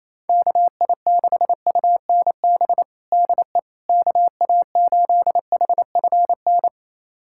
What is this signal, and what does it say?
Morse code, heard as a keyed tone, about 28 wpm, 700 Hz. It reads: KI6UNB DE KA8HFN
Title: USA Callsign DE Callsign - 6 Characters - 28wpm